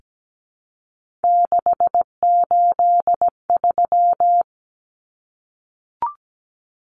Morse code, heard as a keyed tone, about 17 wpm, 700 Hz. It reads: 683
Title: Numbers - 3-Digit 17wpm